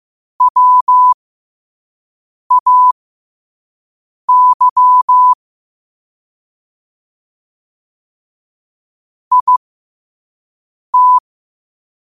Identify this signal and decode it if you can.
Morse code, heard as a keyed tone, about 15 wpm, spaced as for 5 wpm, 1000 Hz. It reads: WAY IT
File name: qotdfiles / 5wpm